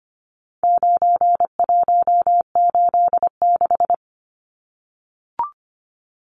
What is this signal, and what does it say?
Morse code, heard as a keyed tone, about 25 wpm, 700 Hz. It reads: 9186
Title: Numbers - 4-Digit 25wpm